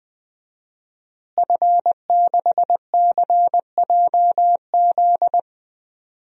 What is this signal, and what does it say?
Morse code, heard as a keyed tone, about 19 wpm, 700 Hz. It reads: F6CJZ
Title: DXCC-100 Call Signs - 5 Characters - Repeated 3x - 20wpm